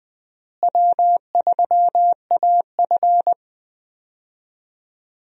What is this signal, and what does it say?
Morse code, heard as a keyed tone, about 20 wpm, 700 Hz. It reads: W3AF
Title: USA Call Signs - 4 Characters - Repeated 2x - 20wpm